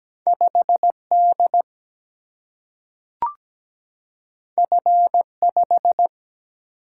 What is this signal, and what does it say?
Morse code, heard as a keyed tone, about 17 wpm, 700 Hz. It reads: 5D F5
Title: Single-Letter Single-Number 17wpm